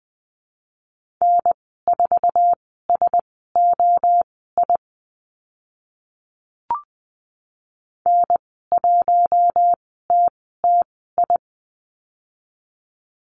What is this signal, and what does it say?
Morse code, heard as a keyed tone, about 20 wpm, spaced as for 14 wpm, 700 Hz. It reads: N4SOI N1TTI
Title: 062 Lesson 11g - Letter h - Call Signs 20wpm